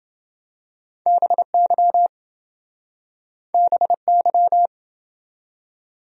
Morse code, heard as a keyed tone, about 31 wpm, 700 Hz. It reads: BY BY